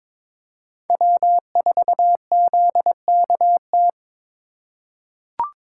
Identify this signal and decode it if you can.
Morse code, heard as a keyed tone, about 22 wpm, 700 Hz. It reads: W4ZKT
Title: USA Call Signs - 5 Characters - 4x Character Spacing - 22wpm